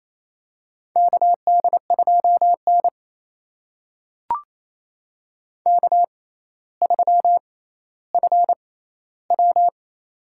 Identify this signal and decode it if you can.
Morse code, heard as a keyed tone, about 28 wpm, spaced as for 14 wpm, 700 Hz. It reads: KD2N K3FW
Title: USA Call Signs - 4 Characters - 6x Character Spacing - 28wpm